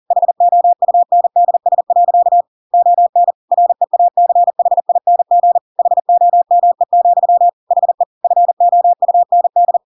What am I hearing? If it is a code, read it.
Morse code, heard as a keyed tone, about 40 wpm, 700 Hz. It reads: HOUNDS. ON REACHING HOME, HE FOUND
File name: Pinocchio / Pinocchio ch4 40 40 ik1hge